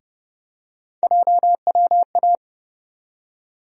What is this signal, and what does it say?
Morse code, heard as a keyed tone, about 30 wpm, 700 Hz. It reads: JWA
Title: USA Call Sign Postfixes - 3 Letters 30wpm